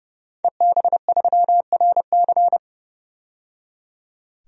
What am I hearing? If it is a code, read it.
Morse code, heard as a keyed tone, about 30 wpm, 700 Hz. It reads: EB3RC
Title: DXCC-100 Call Signs - 5 Characters - 30wpm